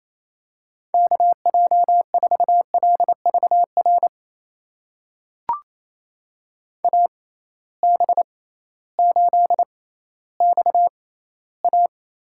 Morse code, heard as a keyed tone, about 28 wpm, spaced as for 15 wpm, 700 Hz. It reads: KJ4LVR AB8XA
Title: USA Call Signs - 6x Character Spacing - 28wpm